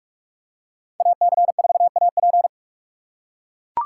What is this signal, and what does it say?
Morse code, heard as a keyed tone, about 43 wpm, 700 Hz. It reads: AC4AP
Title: USA Call Signs - 2x Character Spacing - 45wpm